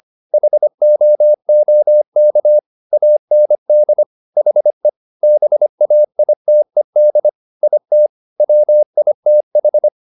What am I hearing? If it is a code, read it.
Morse code, heard as a keyed tone, about 25 wpm, 600 Hz. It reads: HOOK AND HE BAITED IT WITH
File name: the old man and the sea part 1 25wpm